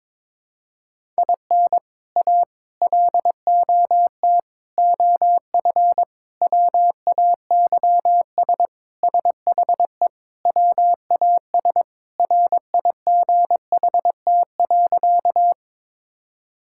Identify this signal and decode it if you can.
Morse code, heard as a keyed tone, about 22 wpm, 700 Hz. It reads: IN A LOT OF WAYS SHE WAS RIGHT.
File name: practice / 22wpm / Sentences with 4-Word Phrases - 22wpm